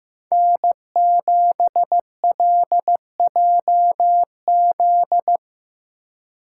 Morse code, heard as a keyed tone, about 15 wpm, 700 Hz. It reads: N7LJZ
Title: USA Call Signs - 5 Characters - 3x Character Spacing - 15wpm